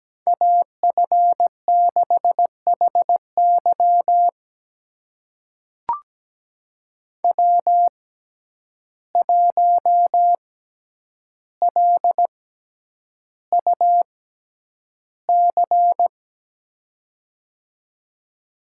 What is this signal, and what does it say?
Morse code, heard as a keyed tone, about 17 wpm, spaced as for 9 wpm, 700 Hz. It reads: AF6HY W1LUC
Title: USA Call Signs - 5 Characters - 6x Character Spacing - 17wpm